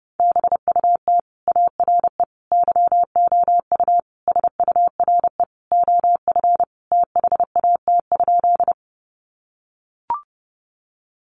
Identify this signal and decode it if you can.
Morse code, heard as a keyed tone, about 28 wpm, 700 Hz. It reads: BUT ARE YOU SURE OF THAT?
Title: Sentences from Top 300 Words 30wpm